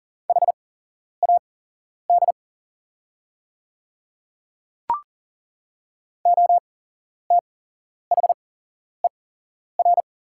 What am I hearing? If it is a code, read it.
Morse code, heard as a keyed tone, about 38 wpm, spaced as for 11 wpm, 700 Hz. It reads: HAD OTHER
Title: 8x Character Spacing - Top 200 Words - 40wpm